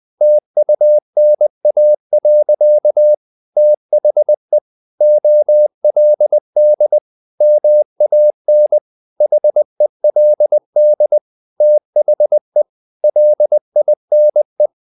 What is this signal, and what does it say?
Morse code, heard as a keyed tone, about 20 wpm, 600 Hz. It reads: TUNA. THE OLD MAN HELD THE LINE
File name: the old man and the sea / the old man and the sea part 1 20wpm